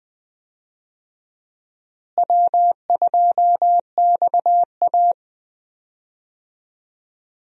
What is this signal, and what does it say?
Morse code, heard as a keyed tone, about 20 wpm, 700 Hz. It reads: W2XA
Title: USA Call Signs - 4 Characters - Repeated 2x - 20wpm